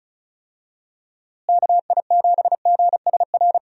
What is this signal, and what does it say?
Morse code, heard as a keyed tone, about 34 wpm, 700 Hz. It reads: KI7GSR
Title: USA Call Signs - 6 Characters - Repeated 3x - 35wpm